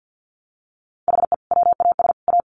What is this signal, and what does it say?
Morse code, heard as a keyed tone, about 48 wpm, 700 Hz. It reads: HE WAS A
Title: Sets of 3 Words 50wpm